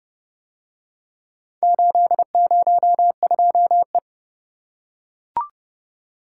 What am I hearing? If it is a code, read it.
Morse code, heard as a keyed tone, about 29 wpm, 700 Hz. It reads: 802E